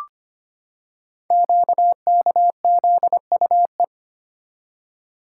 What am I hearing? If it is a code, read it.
Morse code, heard as a keyed tone, about 25 wpm, 700 Hz. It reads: QKZUE
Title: Code Groups - 5 Letters - 25wpm